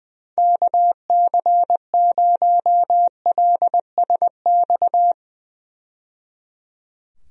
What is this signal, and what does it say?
Morse code, heard as a keyed tone, about 21 wpm, 700 Hz. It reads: KC0LSX
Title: USA Call Signs - 6 Characters - 20wpm